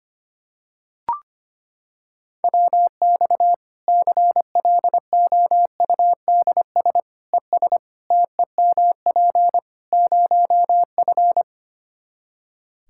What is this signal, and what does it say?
Morse code, heard as a keyed tone, about 25 wpm, 700 Hz. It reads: WX CLOUDS ES TEMP 0F
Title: Practice QSO - Exchange 2 - Info - 25wpm